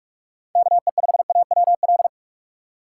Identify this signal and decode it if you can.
Morse code, heard as a keyed tone, about 46 wpm, 700 Hz. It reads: KE5AWL
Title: USA Call Signs - 6 Characters - 3x Character Spacing - 45wpm